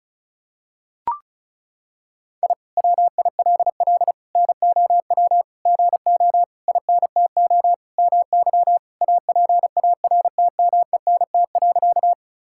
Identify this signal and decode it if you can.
Morse code, heard as a keyed tone, about 34 wpm, 700 Hz. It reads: I WILL NOW GO INTO MY APARTMENT.
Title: Sentences from Top 1000 Words - Review Part 6 - 35wpm